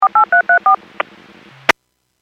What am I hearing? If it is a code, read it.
DTMF, heard as telephone touch tones, tones of about 110 ms. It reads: 48334
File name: WPS / 02713 19-27 WPS 451.25 WPS